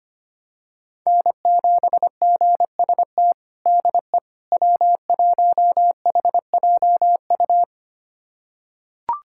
Morse code, heard as a keyed tone, about 25 wpm, 700 Hz. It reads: N7GST DE W1HJU
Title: USA Callsign DE Callsign - 5 Characters - 25wpm